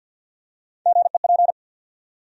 Morse code, heard as a keyed tone, about 53 wpm, 700 Hz. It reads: GEP
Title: USA Call Sign Postfixes - 3 Letters 50wpm